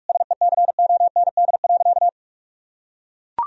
Morse code, heard as a keyed tone, about 41 wpm, 700 Hz. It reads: SECOND.
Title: Sentences from Top 1000 Words - Review Part 1 - 45wpm